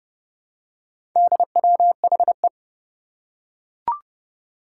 Morse code, heard as a keyed tone, about 29 wpm, 700 Hz. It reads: DWHE
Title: Four Challenging Letter-Number Combinations 30wpm